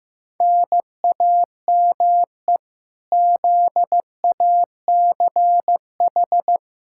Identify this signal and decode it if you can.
Morse code, heard as a keyed tone, about 15 wpm, 700 Hz. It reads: NAME ZACH
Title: K1USN SST QSO Practice 15wpm